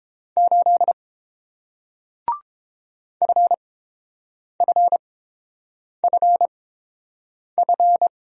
Morse code, heard as a keyed tone, about 29 wpm, spaced as for 7 wpm, 690 Hz. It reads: 8 FFFF